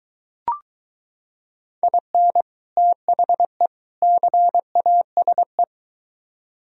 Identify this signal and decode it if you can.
Morse code, heard as a keyed tone, about 22 wpm, 700 Hz. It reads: IN THE CASE